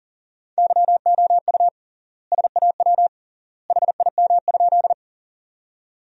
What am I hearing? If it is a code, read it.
Morse code, heard as a keyed tone, about 37 wpm, 700 Hz. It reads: YOU SAW HIM?
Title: 3x Word-Spacing - Sentences from Top 300 Words - 40wpm